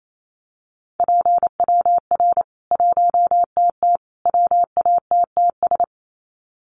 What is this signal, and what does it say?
Morse code, heard as a keyed tone, about 28 wpm, 700 Hz. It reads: PWR 1TT WATTS
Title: Common QSO Phrases 28wpm